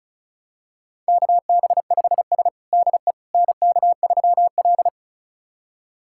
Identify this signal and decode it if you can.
Morse code, heard as a keyed tone, about 34 wpm, 700 Hz. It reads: KB5S DE NK3L
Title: USA Callsign DE Callsign - 4 Characters - 35wpm